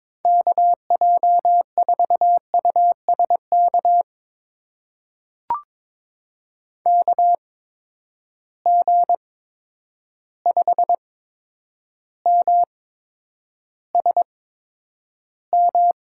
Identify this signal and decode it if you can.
Morse code, heard as a keyed tone, about 21 wpm, spaced as for 9 wpm, 700 Hz. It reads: KJ4USK KG5MSM
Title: USA Call Signs - 6 Characters - 8x Character Spacing - 22wpm